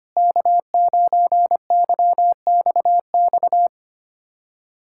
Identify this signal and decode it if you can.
Morse code, heard as a keyed tone, about 25 wpm, 700 Hz. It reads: K9YXX
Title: USA Callsign DE Callsign - 5 Characters - 25wpm